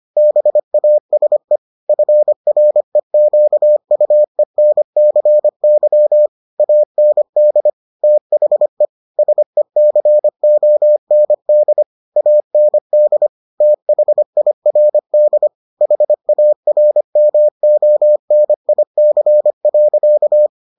Morse code, heard as a keyed tone, about 25 wpm, 600 Hz. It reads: BASE FREQUENCY AND THE SECOND AND THIRD HARMONIC.